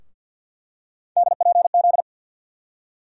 Morse code, heard as a keyed tone, about 48 wpm, 700 Hz. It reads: DPZ
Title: USA Call Sign Postfixes - 3 Letters 50wpm